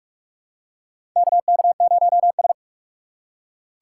Morse code, heard as a keyed tone, about 41 wpm, 700 Hz. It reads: KK0S
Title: USA Call Signs - 4 Characters - 3x Character Spacing - 45wpm